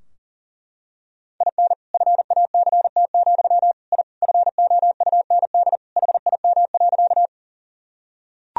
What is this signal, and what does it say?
Morse code, heard as a keyed tone, about 40 wpm, 700 Hz. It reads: IN FACT, I FOUND HIM.
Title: Sentences from Top 1000 Words - Review Part 3 - 40wpm